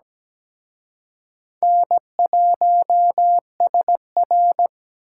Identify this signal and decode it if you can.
Morse code, heard as a keyed tone, about 20 wpm, 700 Hz. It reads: N1SR
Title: Speed Racing - USA 1x2 Call Signs 17wpm